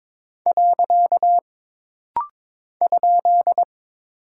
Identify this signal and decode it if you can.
Morse code, heard as a keyed tone, about 22 wpm, 700 Hz. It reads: . ?
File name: practice / 22wpm / Single Punctuation-Prosign - Rapid-Fire - 22wpm